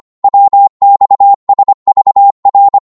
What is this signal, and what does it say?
Morse code, heard as a keyed tone, about 25 wpm, 800 Hz. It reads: WXSVR
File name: Morse / ltr-25wpm